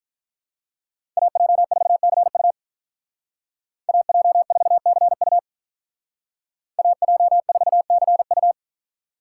Morse code, heard as a keyed tone, about 46 wpm, 700 Hz. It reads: AJ4CU AJ4CU AJ4CU